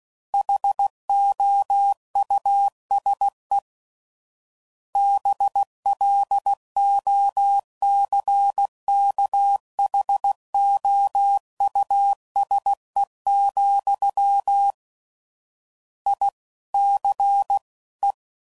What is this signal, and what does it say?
Morse code, heard as a keyed tone, about 16 wpm, 800 Hz. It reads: HOUSE BLOCKHOUSE, I C E
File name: Suffixes / General / Houses